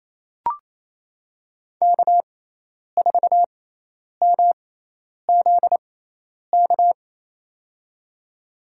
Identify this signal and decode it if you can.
Morse code, heard as a keyed tone, about 28 wpm, spaced as for 10 wpm, 700 Hz. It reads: K4MZK
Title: USA Call Signs - 5 Characters - 6x Character Spacing - 28wpm